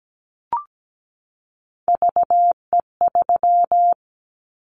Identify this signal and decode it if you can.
Morse code, heard as a keyed tone, about 17 wpm, 700 Hz. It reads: VE3
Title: Callsign Prefixes for Top 100 DXCC - 17wpm